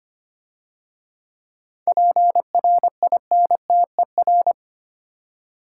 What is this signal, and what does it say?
Morse code, heard as a keyed tone, about 30 wpm, 700 Hz. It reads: PRINTER
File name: Easy Words - Speed-Racing with Hints - 25wpm